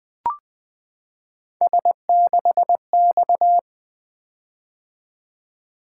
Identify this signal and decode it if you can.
Morse code, heard as a keyed tone, about 20 wpm, 700 Hz. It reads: S6X
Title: Three Letter-Number Combinations - 20wpm